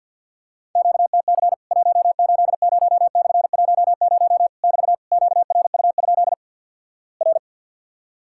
Morse code, heard as a keyed tone, about 50 wpm, 700 Hz. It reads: QTC 180/10 <BT> QRU? R
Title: Worked All Europe DX Contest - QTC Exchange with Repeats - 50wpm